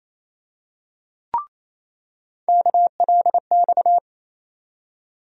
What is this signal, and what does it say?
Morse code, heard as a keyed tone, about 28 wpm, 700 Hz. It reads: KLX
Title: USA Call Sign Postfixes - 3 Letters 28wpm